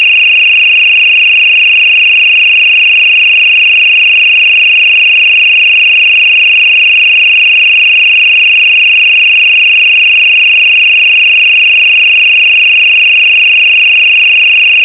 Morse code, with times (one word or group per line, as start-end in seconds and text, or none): none